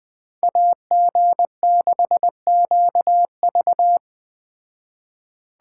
Morse code, heard as a keyed tone, about 21 wpm, 700 Hz. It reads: AG6QV